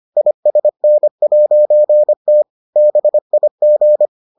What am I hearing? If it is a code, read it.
Morse code, heard as a keyed tone, about 25 wpm, 600 Hz. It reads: ISN'T BIG